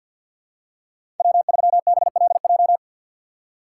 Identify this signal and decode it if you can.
Morse code, heard as a keyed tone, about 48 wpm, 700 Hz. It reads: W3BLJ
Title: USA Call Signs - 6x Character Spacing - 50wpm